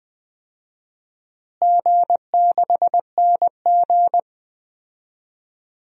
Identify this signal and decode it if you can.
Morse code, heard as a keyed tone, about 20 wpm, 700 Hz. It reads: G6NG
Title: DXCC-100 Call Signs - 4 Characters - Repeated 2x - 20wpm